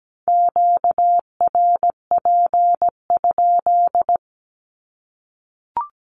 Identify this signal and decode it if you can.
Morse code, heard as a keyed tone, about 17 wpm, 700 Hz. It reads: QRP?
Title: Q-Codes 17wpm